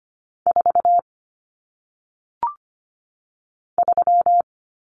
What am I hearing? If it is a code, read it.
Morse code, heard as a keyed tone, about 25 wpm, 700 Hz. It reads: 4 3
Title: Numbers - 1-Digit 25wpm